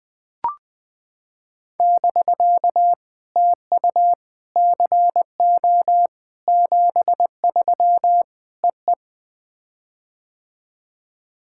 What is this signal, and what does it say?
Morse code, heard as a keyed tone, about 20 wpm, 660 Hz. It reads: <BK> TU CO 73 EE
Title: POTA QSO Practice - 20wpm